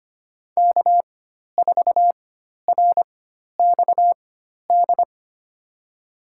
Morse code, heard as a keyed tone, about 25 wpm, spaced as for 12 wpm, 700 Hz. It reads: K4RXD
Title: USA Call Signs - 4x Character Spacing - 25wpm